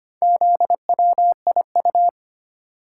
Morse code, heard as a keyed tone, about 25 wpm, 700 Hz. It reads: ZWIU
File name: practice / 25wpm / Four Letter-Number Combinations - 25wpm